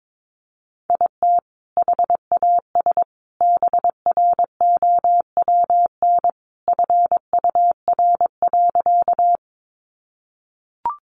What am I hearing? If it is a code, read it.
Morse code, heard as a keyed tone, about 22 wpm, 700 Hz. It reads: IT HAS BROWN FUR.
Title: Easy Phrases 22wpm